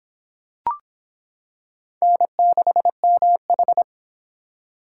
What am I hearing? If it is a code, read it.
Morse code, heard as a keyed tone, about 25 wpm, 700 Hz. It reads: N6MH